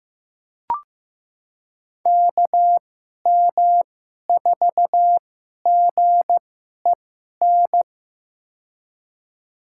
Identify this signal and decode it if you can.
Morse code, heard as a keyed tone, about 15 wpm, spaced as for 11 wpm, 700 Hz. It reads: KM4GEN
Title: USA Call Signs - 6 Characters - 2x Character Spacing - 15wpm